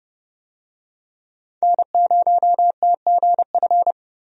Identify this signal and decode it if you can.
Morse code, heard as a keyed tone, about 31 wpm, 700 Hz. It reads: N0TGF
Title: USA Call Signs - 5 Characters - 30wpm